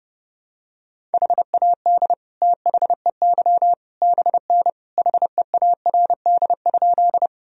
Morse code, heard as a keyed tone, about 30 wpm, 700 Hz. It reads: HAD THEY BN HEARD?
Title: Sentences from Top 400 Words and Abbreviations 30wpm